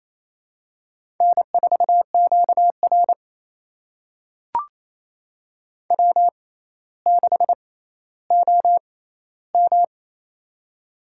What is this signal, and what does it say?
Morse code, heard as a keyed tone, about 29 wpm, spaced as for 15 wpm, 700 Hz. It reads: N4QR W6OM
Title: USA Call Signs - 4 Characters - 6x Character Spacing - 28wpm